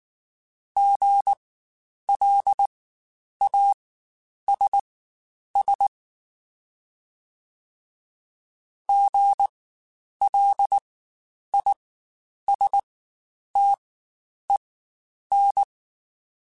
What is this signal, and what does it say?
Morse code, heard as a keyed tone, about 19 wpm, spaced as for 9 wpm, 800 Hz. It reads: GLASS GLISTEN
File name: Spaced Alphas / Spaced Alphas mp3s / Gs sp @18